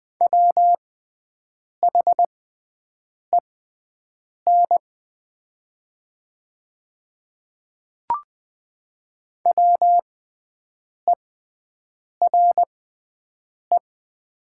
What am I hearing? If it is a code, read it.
Morse code, heard as a keyed tone, about 20 wpm, spaced as for 7 wpm, 700 Hz. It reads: WHEN WERE
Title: 6x Character Spacing - Top 100 Words - 20wpm